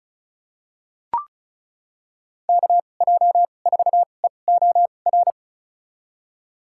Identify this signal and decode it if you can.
Morse code, heard as a keyed tone, about 34 wpm, spaced as for 25 wpm, 700 Hz. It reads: KJ4EOR
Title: USA Call Signs - 6 Characters - 2x Character Spacing - 35wpm